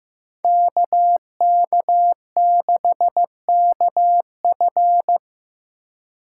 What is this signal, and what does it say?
Morse code, heard as a keyed tone, about 15 wpm, 700 Hz. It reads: KK6KF
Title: USA Call Signs - 5 Characters - 4x Character Spacing - 15wpm